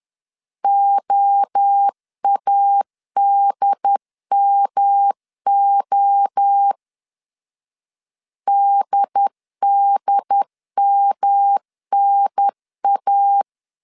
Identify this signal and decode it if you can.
Morse code, heard as a keyed tone, about 10 wpm, 800 Hz. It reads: OADMO DDMNA
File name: CW / DET / 1 3 1500C S C 10wpm 3335